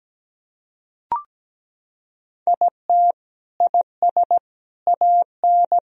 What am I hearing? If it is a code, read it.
Morse code, heard as a keyed tone, about 17 wpm, 700 Hz. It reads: IT IS AN
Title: Sentences from Top 1000 Words - Review Part 4 - 17wpm